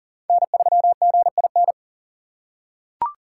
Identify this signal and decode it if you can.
Morse code, heard as a keyed tone, about 40 wpm, 700 Hz. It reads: N3GIN
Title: USA Call Signs - 5 Characters - 4x Character Spacing - 40wpm